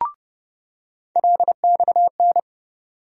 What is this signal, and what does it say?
Morse code, heard as a keyed tone, about 30 wpm, 700 Hz. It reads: LXN
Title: USA Call Sign Postfixes - 3 Letters 30wpm